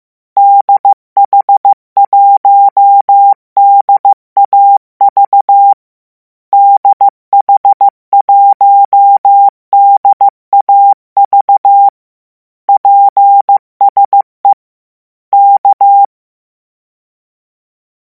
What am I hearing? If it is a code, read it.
Morse code, heard as a keyed tone, about 15 wpm, 800 Hz. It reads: DH1DAV DH1DAV PSE K